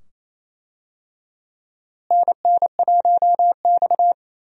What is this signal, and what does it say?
Morse code, heard as a keyed tone, about 28 wpm, 700 Hz. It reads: NN1X
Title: USA Call Signs - 4 Characters - Repeated 2x - 28wpm